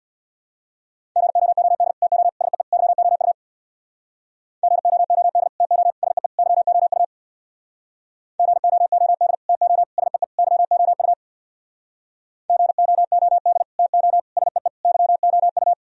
Morse code, heard as a keyed tone, about 64 wpm, 700 Hz. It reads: GOOD TO SEE YOU GOOD TO SEE YOU GOOD TO SEE YOU GOOD TO SEE YOU